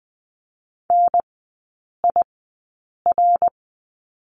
Morse code, heard as a keyed tone, about 20 wpm, spaced as for 8 wpm, 700 Hz. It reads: NIR
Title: 093 Lesson 16e - Letter c - Three Characters 20wpm